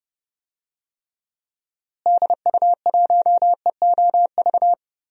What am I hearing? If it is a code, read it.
Morse code, heard as a keyed tone, about 30 wpm, 700 Hz. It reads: DU1EOV
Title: DXCC-100 Call Signs - 6 Characters - Repeated 2x - 30wpm